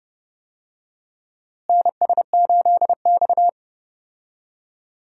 Morse code, heard as a keyed tone, about 30 wpm, 700 Hz. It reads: NS8X
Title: USA Call Signs - 4 Characters - Repeated 2x - 30wpm